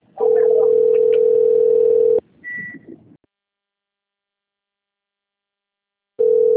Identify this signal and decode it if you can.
Ringback tone (North America), 440 + 480 Hz.
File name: rg-1600-unknown- - - .27251